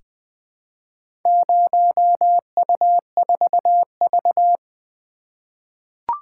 Morse code, heard as a keyed tone, about 20 wpm, 700 Hz. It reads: 0U4V
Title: Four Letter-Number Combinations - 20wpm